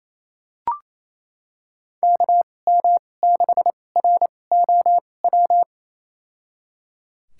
Morse code, heard as a keyed tone, about 28 wpm, spaced as for 20 wpm, 700 Hz. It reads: KM6ROW
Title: USA Call Signs - 2x Character Spacing - 28wpm